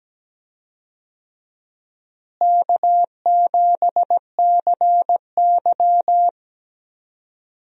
Morse code, heard as a keyed tone, about 17 wpm, 700 Hz. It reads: K7CY
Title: USA Call Signs - 4 Characters - Repeated 3x - 17wpm